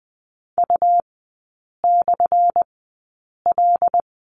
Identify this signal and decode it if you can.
Morse code, heard as a keyed tone, about 20 wpm, spaced as for 8 wpm, 700 Hz. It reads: U/L
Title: 205 Lesson 33e - Letter j - Three Characters 20wpm